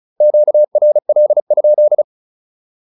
Morse code, heard as a keyed tone, about 36 wpm, 600 Hz. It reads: QRL?
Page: advanced-practice-files